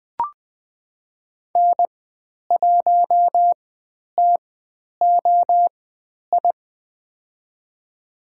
Morse code, heard as a keyed tone, about 20 wpm, spaced as for 10 wpm, 700 Hz. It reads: N1TOI
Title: 095 Lesson 16g - Letter c - Call Signs 20wpm